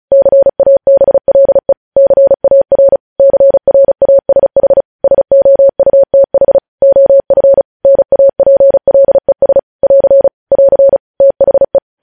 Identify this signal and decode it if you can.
Morse code, heard as a keyed tone, about 36 wpm, 550 Hz. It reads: CABLE CAR CRASH SOUTH OF NAPLES <AR> <AR> THE
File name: podcast / News headlines-1x-35x35